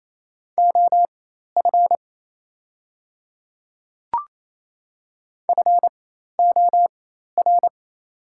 Morse code, heard as a keyed tone, about 28 wpm, spaced as for 13 wpm, 700 Hz. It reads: OF FOR
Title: 4x Character Spacing - Top 100 Words - 28wpm